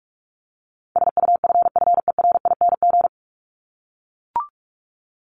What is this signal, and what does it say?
Morse code, heard as a keyed tone, about 44 wpm, 700 Hz. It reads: SUFFERING
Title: ING Suffix - Top 100 Words with Base Words 45wpm